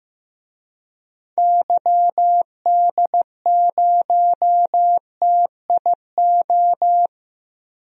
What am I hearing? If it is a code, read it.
Morse code, heard as a keyed tone, about 15 wpm, 700 Hz. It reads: YD0TIO
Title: DXCC-100 Call Signs - 6 Characters - 15wpm